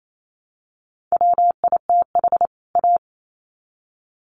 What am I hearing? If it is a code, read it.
Morse code, heard as a keyed tone, about 28 wpm, 700 Hz. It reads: WITH A
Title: Sets of 2 Words 28wpm